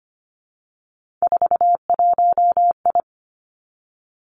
Morse code, heard as a keyed tone, about 25 wpm, 700 Hz. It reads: 41I
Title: Three Challenging Letter-Number Combinations 25wpm